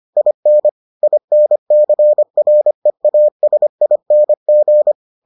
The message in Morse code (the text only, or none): IN INCREASING